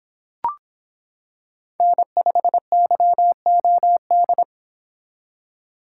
Morse code, heard as a keyed tone, about 26 wpm, 700 Hz. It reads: N5YOD